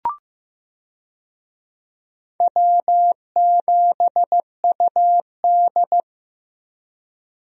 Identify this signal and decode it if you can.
Morse code, heard as a keyed tone, about 16 wpm, 700 Hz. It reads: W7UD
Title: USA Call Signs - 4 Characters - Repeated 2x - 15wpm